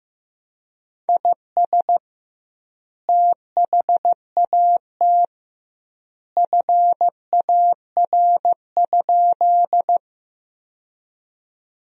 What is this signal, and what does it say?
Morse code, heard as a keyed tone, about 16 wpm, 700 Hz. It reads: IS THAT FAR?
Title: 2x Word-Spacing - Sentences from Top 300 Words - 15wpm